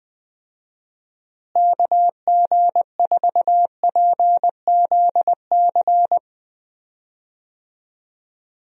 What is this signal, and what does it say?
Morse code, heard as a keyed tone, about 19 wpm, 700 Hz. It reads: KG4PZC
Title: USA Call Signs - 6 Characters - Repeated 3x - 20wpm